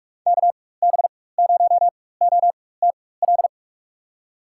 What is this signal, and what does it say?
Morse code, heard as a keyed tone, about 45 wpm, spaced as for 21 wpm, 700 Hz. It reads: KB0OTL
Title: USA Call Signs - 6 Characters - 4x Character Spacing - 45wpm